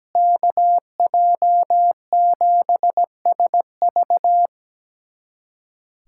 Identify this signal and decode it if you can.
Morse code, heard as a keyed tone, about 17 wpm, 700 Hz. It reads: KJ7SV
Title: USA Callsign DE Callsign - 5 Characters - 17wpm